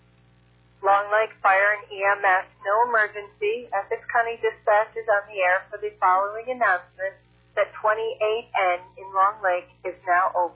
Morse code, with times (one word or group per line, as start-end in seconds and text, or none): none